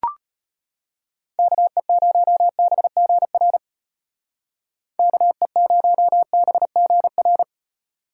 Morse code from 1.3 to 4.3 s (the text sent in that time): KE0BGR